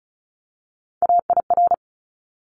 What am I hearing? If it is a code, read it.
Morse code, heard as a keyed tone, about 34 wpm, 700 Hz. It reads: AIR
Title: Top 200 Words Repeated 35wpm